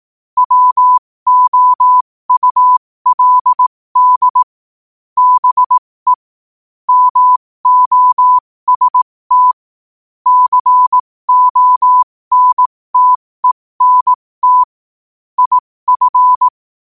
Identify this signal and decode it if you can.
Morse code, heard as a keyed tone, about 18 wpm, spaced as for 15 wpm, 1000 Hz. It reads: WOULD BE MOST CONTENT IF
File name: qotdfiles / 15wpm